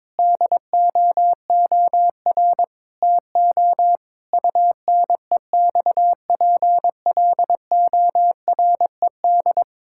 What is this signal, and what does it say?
Morse code, heard as a keyed tone, about 22 wpm, 700 Hz. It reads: DOOR TO UNEXPLORED
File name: Difficult Sentences with Hints - 22wpm